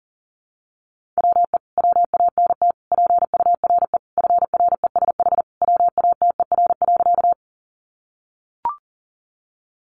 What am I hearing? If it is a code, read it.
Morse code, heard as a keyed tone, about 40 wpm, 700 Hz. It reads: WE WANT PURE FRESH WATER.
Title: Sentences from Top 1500 Words 40wpm